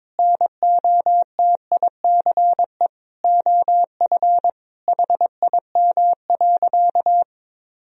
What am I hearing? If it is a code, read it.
Morse code, heard as a keyed tone, about 22 wpm, 700 Hz. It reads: NOTICE OF HIM.
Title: Sentences from Top 1000 Words - Review Part 4 - 22wpm